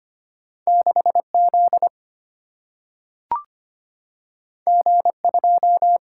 Morse code, heard as a keyed tone, about 25 wpm, 700 Hz. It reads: 6Z G2
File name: Two Letter-Number Combinations - 25wpm